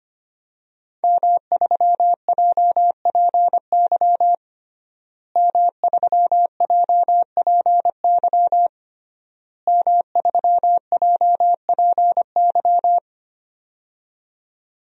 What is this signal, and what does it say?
Morse code, heard as a keyed tone, about 25 wpm, 700 Hz. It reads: M3JPY M3JPY M3JPY
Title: DXCC-100 Call Signs - 5 Characters - Repeated 3x - 25wpm